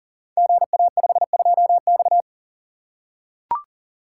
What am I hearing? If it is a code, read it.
Morse code, heard as a keyed tone, about 39 wpm, 700 Hz. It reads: GA52X
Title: Code Groups - 5 Letter:Numbers - 40wpm